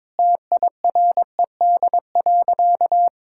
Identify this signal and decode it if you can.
Morse code, heard as a keyed tone, about 22 wpm, 700 Hz. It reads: TIRED.
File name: practice / 22wpm / Sentences with 3-Word Phrases - 22wpm